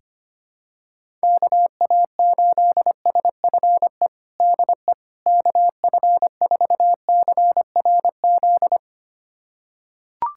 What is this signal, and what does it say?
Morse code, heard as a keyed tone, about 25 wpm, 700 Hz. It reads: KA8SFE DE KF4CRZ
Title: USA Callsign DE Callsign - 6 Characters - 25wpm